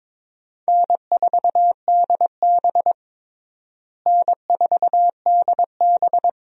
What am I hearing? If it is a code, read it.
Morse code, heard as a keyed tone, about 22 wpm, 700 Hz. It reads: N4DB N4DB
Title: USA Call Signs - 4 Characters - Repeated 2x - 22wpm